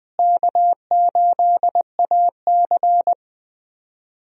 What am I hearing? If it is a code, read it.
Morse code, heard as a keyed tone, about 19 wpm, 700 Hz. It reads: K8AC
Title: USA Call Signs - 4 Characters - Repeated 2x - 20wpm